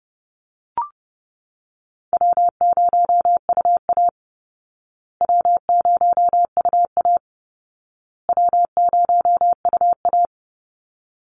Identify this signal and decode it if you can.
Morse code, heard as a keyed tone, about 30 wpm, 700 Hz. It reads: W0UA W0UA W0UA
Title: Easy Call Signs Repeated 3x - 30wpm